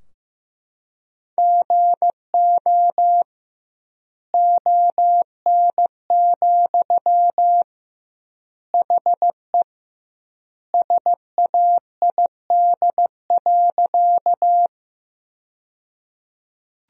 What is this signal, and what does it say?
Morse code, heard as a keyed tone, about 15 wpm, 700 Hz. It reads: GO ON, HE SAID.
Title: 2x Word-Spacing - Sentences from Top 100 Words - 15wpm